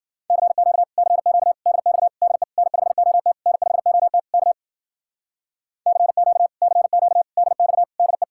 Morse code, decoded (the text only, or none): CQ CQ DX DE N5OT N5OT K CQ CQ DX DE